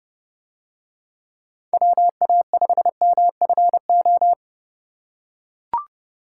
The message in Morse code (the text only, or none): WA5MFO